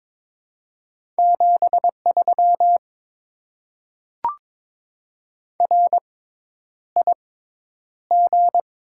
Morse code, heard as a keyed tone, about 22 wpm, spaced as for 10 wpm, 700 Hz. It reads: 73 RIG
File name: Top 20 QSO Elements - 6x Character Spacing - 22wpm